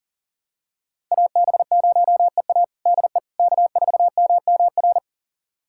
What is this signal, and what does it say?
Morse code, heard as a keyed tone, about 40 wpm, 700 Hz. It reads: AB0EA DE K4MMR
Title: USA Callsign DE Callsign - 5 Characters - 40wpm